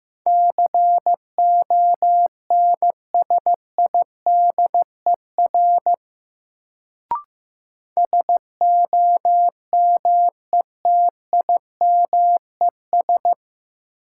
Morse code, heard as a keyed tone, about 15 wpm, 700 Hz. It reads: CONSIDER SOMETIMES
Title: Sentences from Top 600 Words - New Word Drill - 15wpm